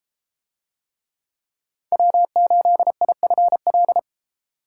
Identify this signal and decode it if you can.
Morse code, heard as a keyed tone, about 31 wpm, 700 Hz. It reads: W8IFL